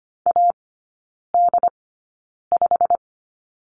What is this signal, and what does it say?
Morse code, heard as a keyed tone, about 25 wpm, spaced as for 9 wpm, 700 Hz. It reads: AD5
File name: course / 25:10wpm / 093 Lesson 16e - Letter c - Three Characters 25wpm